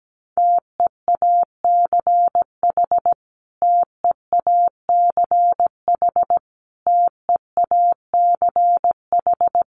Morse code, heard as a keyed tone, about 17 wpm, 700 Hz. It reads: TEACH TEACH TEACH
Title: Top 500 Words Repeated 17wpm